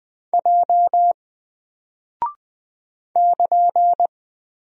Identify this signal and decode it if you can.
Morse code, heard as a keyed tone, about 20 wpm, 700 Hz. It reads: J <KN>
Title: Single Character (Letter:Number:Punctuation:Prosign) Alt 20wpm